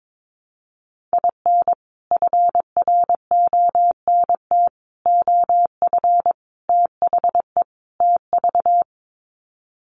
Morse code, heard as a keyed tone, about 22 wpm, 700 Hz. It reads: IN FRONT OF THE TV
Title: Sets of 5 Words Encore2 22wpm